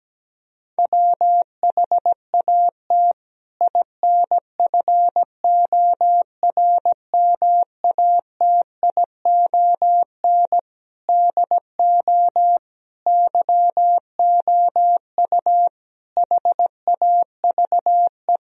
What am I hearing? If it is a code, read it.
Morse code, heard as a keyed tone, about 17 wpm, 700 Hz. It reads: WHAT INFORMATION DO YOU HAVE
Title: Sentences from Top 1000 Words - Review Part 3 - 17wpm